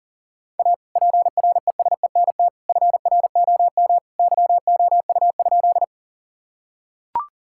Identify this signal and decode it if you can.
Morse code, heard as a keyed tone, about 40 wpm, 700 Hz. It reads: A PRESENT FROM YOU?
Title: Sentences from Top 1000 Words - Review Part 5 - 40wpm